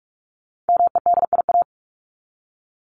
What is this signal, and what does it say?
Morse code, heard as a keyed tone, about 44 wpm, 700 Hz. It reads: MEDIA
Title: Top 501-750 Words 45wpm